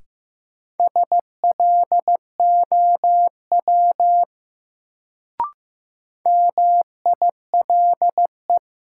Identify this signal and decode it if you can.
Morse code, heard as a keyed tone, about 15 wpm, 700 Hz. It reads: SLOW MILE
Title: Sentences from Top 400 Words - New Word Drill - 15wpm